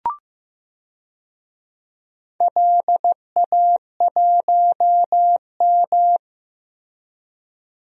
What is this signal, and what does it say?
Morse code, heard as a keyed tone, about 15 wpm, 700 Hz. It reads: LA1M
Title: DXCC-100 Call Signs - 4 Characters - Repeated 3x - 15wpm